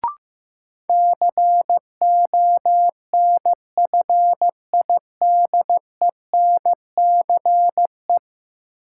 Morse code, heard as a keyed tone, about 15 wpm, 700 Hz. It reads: CONFIDENCE
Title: Sentences from Top 1500 Words - New Word Drill - 15wpm